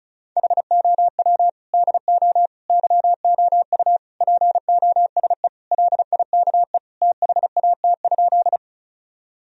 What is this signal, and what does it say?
Morse code, heard as a keyed tone, about 34 wpm, 700 Hz. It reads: HOW DO YOU POSE LIKE THAT?
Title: Sentences from Top 1000 Words - Review Part 2 - 35wpm